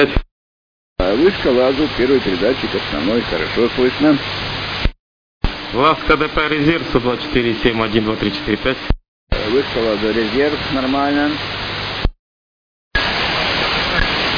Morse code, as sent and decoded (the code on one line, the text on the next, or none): none